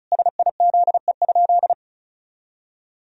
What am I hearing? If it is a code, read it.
Morse code, heard as a keyed tone, about 34 wpm, 700 Hz. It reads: SIZE?
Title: Sentences from Top 500 Words and Abbreviations 35wpm